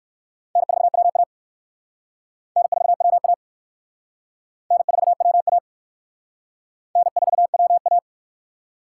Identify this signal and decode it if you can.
Morse code, heard as a keyed tone, about 54 wpm, 700 Hz. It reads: N4WA N4WA N4WA N4WA
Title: Speed Racing - USA 1x2 Call Signs 45wpm